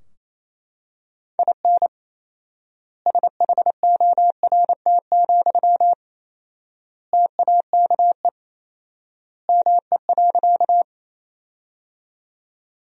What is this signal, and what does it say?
Morse code, heard as a keyed tone, about 27 wpm, 700 Hz. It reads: IN SHORT, TAKE ME.
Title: 4x Word-Spacing - Sentences from Top 400 Words - 28wpm